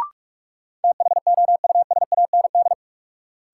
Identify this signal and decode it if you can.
Morse code, heard as a keyed tone, about 42 wpm, 700 Hz. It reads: THOUSAND
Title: Sentences from Top 600 Words - New Word Drill - 45wpm